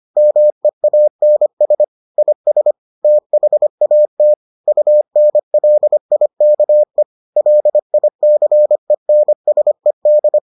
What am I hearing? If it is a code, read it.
Morse code, heard as a keyed tone, about 25 wpm, 600 Hz. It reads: MEANS IS THAT UNLIKE LICENSED